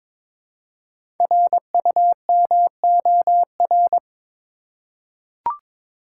Morse code, heard as a keyed tone, about 22 wpm, 700 Hz. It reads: RUMOR
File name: 5 Letter Words - First 3 Letters Given - 22wpm